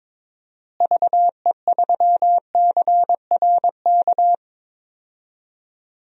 Morse code, heard as a keyed tone, about 21 wpm, 700 Hz. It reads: VE3CRK
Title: DXCC-100 Call Signs - 6 Characters - Repeated 3x - 22wpm